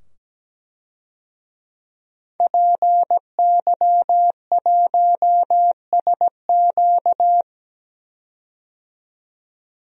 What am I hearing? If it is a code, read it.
Morse code, heard as a keyed tone, about 17 wpm, 700 Hz. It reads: PY1SQ
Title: DXCC-100 Call Signs - 5 Characters - 17wpm